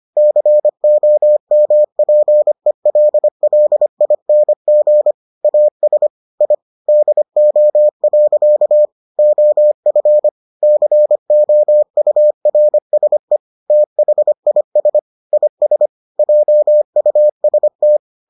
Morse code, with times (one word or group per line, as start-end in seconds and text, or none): 0.2-5.1 COMPELLING
5.4-6.1 AS
6.4-6.5 I
6.9-8.8 DO.
9.2-10.3 OF
10.6-13.4 COURSE
13.7-15.0 THIS
15.3-15.9 IS
16.2-18.0 JUST